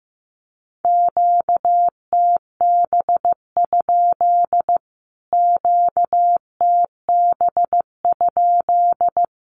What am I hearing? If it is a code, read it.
Morse code, heard as a keyed tone, about 15 wpm, 700 Hz. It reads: QTB? QTB?
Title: Q-Codes Repeated 3x 15wpm